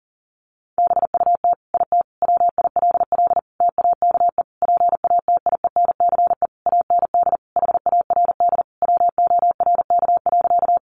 Morse code, heard as a keyed tone, about 40 wpm, 700 Hz. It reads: BUT IT WILL TAKE PATIENCE AND HARD WORK.
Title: Sentences from Top 1300 Words 40wpm